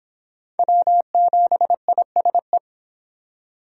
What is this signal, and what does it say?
Morse code, heard as a keyed tone, about 23 wpm, 700 Hz. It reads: W7ISE